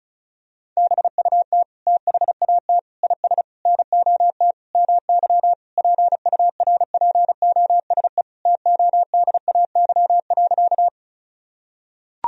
Morse code, heard as a keyed tone, about 36 wpm, 700 Hz. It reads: BUT THAT IS NOT MY PURPOSE TODAY.
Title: Sentences from Top 1000 Words - Review Part 4 - 35wpm